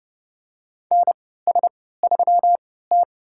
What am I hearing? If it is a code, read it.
Morse code, heard as a keyed tone, about 30 wpm, spaced as for 17 wpm, 700 Hz. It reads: NS3T
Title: USA Call Signs - 3x Character Spacing - 30wpm